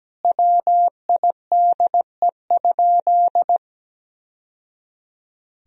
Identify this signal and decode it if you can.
Morse code, heard as a keyed tone, about 17 wpm, 700 Hz. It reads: WIDE?
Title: Sentences from Top 1000 Words - Review Part 3 - 17wpm